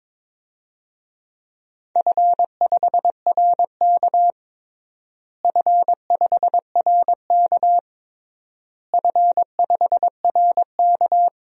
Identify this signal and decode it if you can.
Morse code, heard as a keyed tone, about 22 wpm, 700 Hz. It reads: F5RK F5RK F5RK
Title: DXCC-100 Call Signs - 4 Characters - Repeated 3x - 22wpm